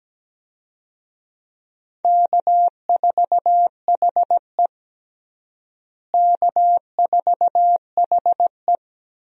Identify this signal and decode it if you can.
Morse code, heard as a keyed tone, about 17 wpm, 700 Hz. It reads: K4HE K4HE
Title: USA Call Signs - 4 Characters - Repeated 2x - 17wpm